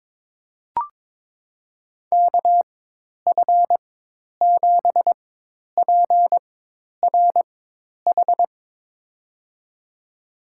Morse code, heard as a keyed tone, about 22 wpm, spaced as for 10 wpm, 700 Hz. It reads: KF7PRH
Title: USA Call Signs - 6 Characters - 4x Character Spacing - 22wpm